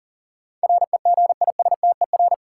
Morse code, heard as a keyed tone, about 40 wpm, 700 Hz. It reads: REGISTER
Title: Sentences from Top 1200 Words - New Word Drill - 40wpm